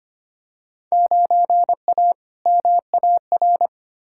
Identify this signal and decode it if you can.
Morse code, heard as a keyed tone, about 25 wpm, 700 Hz. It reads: 9A MAR
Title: Field Day Exchanges Encore - 25wpm